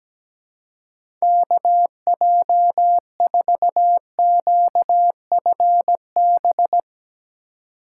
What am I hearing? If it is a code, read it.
Morse code, heard as a keyed tone, about 17 wpm, 700 Hz. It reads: KJ4QFB
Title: USA Call Signs - 2x Character Spacing - 17wpm